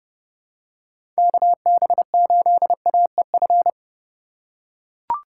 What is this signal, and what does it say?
Morse code, heard as a keyed tone, about 30 wpm, 700 Hz. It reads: KB8AEF
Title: USA Call Signs - 6 Characters - 3x Character Spacing - 30wpm